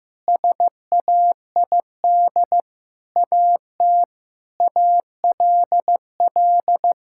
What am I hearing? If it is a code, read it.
Morse code, heard as a keyed tone, about 15 wpm, 700 Hz. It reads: SAID AT ALL
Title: Sentences from Top 1000 Words - Review Part 5 - 15wpm